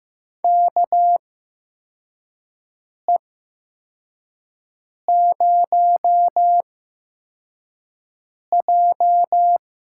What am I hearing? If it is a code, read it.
Morse code, heard as a keyed tone, about 15 wpm, spaced as for 4 wpm, 700 Hz. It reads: KE0J
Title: USA Call Signs - 8x Character Spacing - 15wpm